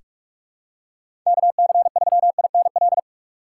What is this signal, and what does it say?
Morse code, heard as a keyed tone, about 44 wpm, 700 Hz. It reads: KC3INL
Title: USA Call Signs - 6 Characters - 3x Character Spacing - 45wpm